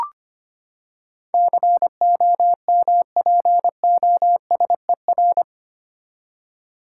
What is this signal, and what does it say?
Morse code, heard as a keyed tone, about 25 wpm, 700 Hz. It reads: COMPOSER
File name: Eight Letter Words - 25wpm